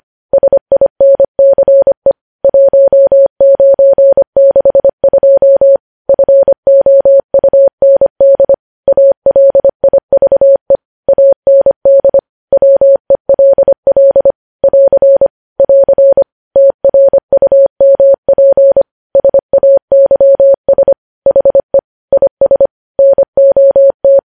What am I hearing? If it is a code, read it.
Morse code, heard as a keyed tone, about 25 wpm, 550 Hz. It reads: SINCE 1962 FOUND ALIVE AND WELL <AR> <AR> TRUMP SAYS HE IS NOT
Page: News Headlines in Morse Code at 25 WPM